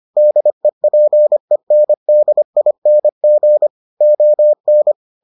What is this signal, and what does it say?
Morse code, heard as a keyed tone, about 25 wpm, 600 Hz. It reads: DEPENDING ON